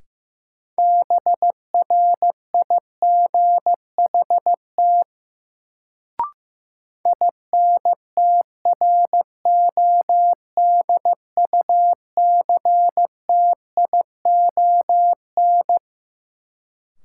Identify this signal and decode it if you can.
Morse code, heard as a keyed tone, about 15 wpm, 700 Hz. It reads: BRIGHT INTRODUCTION
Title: Sentences from Top 1000 Words - New Word Drill - 15wpm